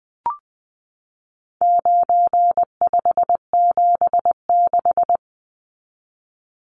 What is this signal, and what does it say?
Morse code, heard as a keyed tone, about 20 wpm, 700 Hz. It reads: 9576
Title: Numbers - 4-Digit 20wpm